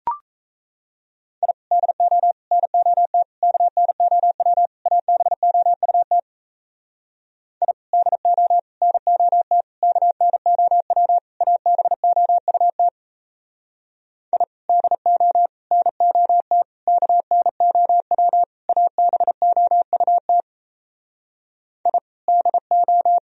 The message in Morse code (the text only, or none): I DO NOT KNOW ABOUT I DO NOT KNOW ABOUT I DO NOT KNOW ABOUT I DO